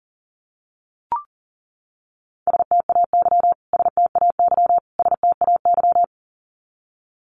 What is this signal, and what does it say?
Morse code, heard as a keyed tone, about 40 wpm, 700 Hz. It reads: STAY STAY STAY
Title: Top 500 Words Repeated 40wpm